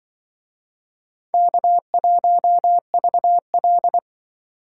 Morse code, heard as a keyed tone, about 25 wpm, 700 Hz. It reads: K1VL
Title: Speed Racing - USA 1x2 Call Signs 20wpm